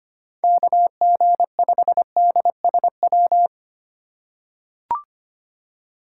Morse code, heard as a keyed tone, about 25 wpm, 700 Hz. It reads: KG5DSW
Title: USA Call Signs - 4x Character Spacing - 25wpm